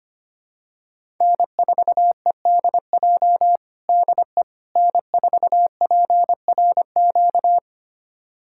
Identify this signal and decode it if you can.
Morse code, heard as a keyed tone, about 25 wpm, 700 Hz. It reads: N4EDJ DE N4PRQ
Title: USA Callsign DE Callsign - 5 Characters - 25wpm